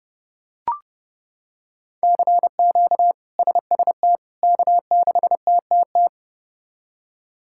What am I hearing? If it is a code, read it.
Morse code, heard as a keyed tone, about 30 wpm, 700 Hz. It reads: CQ SST K6TTT
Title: K1USN SST QSO Practice - 30wpm